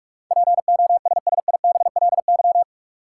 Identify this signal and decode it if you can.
Morse code, heard as a keyed tone, about 47 wpm, 700 Hz. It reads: POSSIBLY
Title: Sentences from Top 1300 Words - New Word Drill - 45wpm